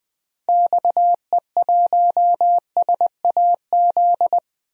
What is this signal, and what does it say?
Morse code, heard as a keyed tone, about 21 wpm, 700 Hz. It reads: XE1SAZ
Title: DXCC-100 Call Signs - 6 Characters - 20wpm